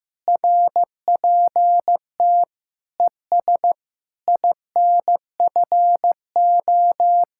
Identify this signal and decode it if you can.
Morse code, heard as a keyed tone, about 15 wpm, 700 Hz. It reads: RPT ES INFO
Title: Speed-Racing Common QSO Phrases Encore - 15wpm